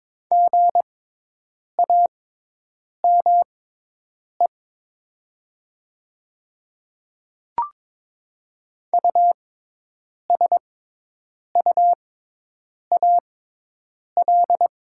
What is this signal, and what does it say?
Morse code, heard as a keyed tone, about 22 wpm, spaced as for 8 wpm, 700 Hz. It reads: GAME USUAL
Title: 6x Character Spacing - Top 500 Words - 22wpm